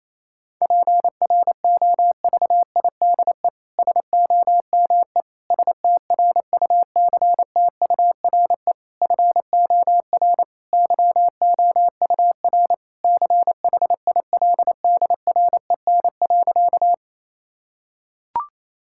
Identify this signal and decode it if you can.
Morse code, heard as a keyed tone, about 28 wpm, 700 Hz. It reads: PROVIDE SOME STRUCTURE FOR YOUR CHILDREN.
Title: Sentences from Top 1000 Words - Review Part 6 - 28wpm